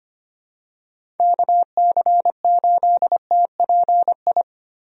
Morse code, heard as a keyed tone, about 25 wpm, 700 Hz. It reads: KC8TPI
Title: USA Call Signs - 6 Characters - Repeated 2x - 25wpm